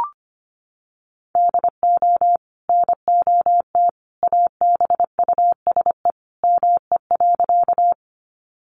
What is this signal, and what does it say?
Morse code, heard as a keyed tone, about 25 wpm, 700 Hz. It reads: DO NOT ABUSE ME.
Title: Sentences from Top 1300 Words 25wpm